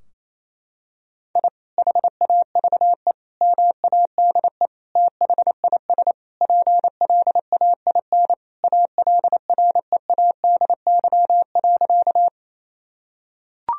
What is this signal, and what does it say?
Morse code, heard as a keyed tone, about 28 wpm, 700 Hz. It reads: I HAVE MADE THIS PLAIN ALREADY.
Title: Sentences from Top 1000 Words - Review Part 2 - 28wpm